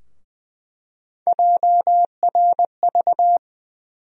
Morse code, heard as a keyed tone, about 20 wpm, 700 Hz. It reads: JRV
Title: USA Call Sign Postfixes - 3 Letters 20wpm